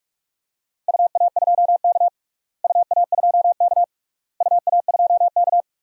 Morse code, heard as a keyed tone, about 46 wpm, 700 Hz. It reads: UA2K UA2K UA2K
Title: DXCC-100 Call Signs - 4 Characters - Repeated 3x - 45wpm